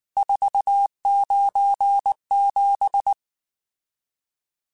Morse code, heard as a keyed tone, about 19 wpm, 800 Hz. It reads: 497
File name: RSTs / RST-mp3s / Various RST @18